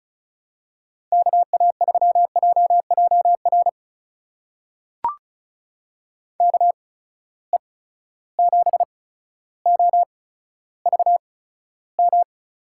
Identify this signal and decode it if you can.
Morse code, heard as a keyed tone, about 36 wpm, spaced as for 15 wpm, 700 Hz. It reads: KA3JJR KE7OVM
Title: USA Call Signs - 6 Characters - 8x Character Spacing - 35wpm